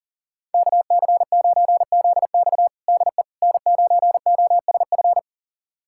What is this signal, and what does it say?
Morse code, heard as a keyed tone, about 40 wpm, 690 Hz. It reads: KC9ZX DE N9OSF
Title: USA Callsign DE Callsign - 5 Characters - 40wpm